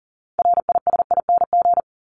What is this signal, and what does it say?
Morse code, heard as a keyed tone, about 40 wpm, 700 Hz. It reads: RISING
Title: ING Suffix - Top 100 Words with Base Words 40wpm